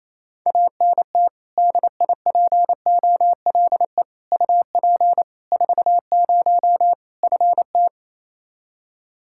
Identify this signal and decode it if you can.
Morse code, heard as a keyed tone, about 28 wpm, 700 Hz. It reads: ANT DIPOLE UP 40 FT
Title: Practice QSO - Exchange 2 - Info - 28wpm